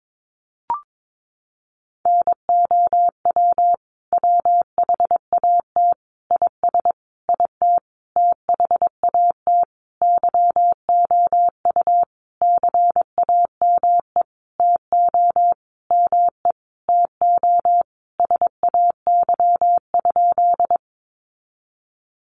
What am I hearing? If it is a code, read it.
Morse code, heard as a keyed tone, about 22 wpm, 700 Hz. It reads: NOW WHAT IS IT THAT YOU CAME TO ME TO SAY?
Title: Sentences from Top 200 Words 22wpm